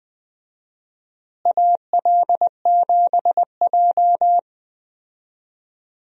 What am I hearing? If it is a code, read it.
Morse code, heard as a keyed tone, about 20 wpm, 700 Hz. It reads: AL7J
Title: USA Call Signs - 4 Characters - Repeated 2x - 20wpm